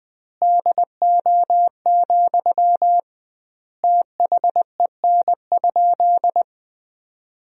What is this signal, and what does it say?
Morse code, heard as a keyed tone, about 20 wpm, 700 Hz. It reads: DO, THEN?
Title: 2x Word-Spacing - Sentences from Top 100 Words - 20wpm